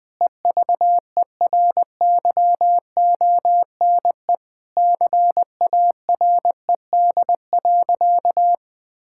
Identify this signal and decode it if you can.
Morse code, heard as a keyed tone, about 20 wpm, 700 Hz. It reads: EVERYONE CARED.
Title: Sentences with 8-Word Phrases - 20wpm